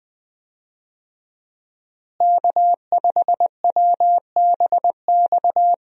Morse code, heard as a keyed tone, about 20 wpm, 700 Hz. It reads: K5WBX
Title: USA Call Signs - 5 Characters - Repeated 2x - 20wpm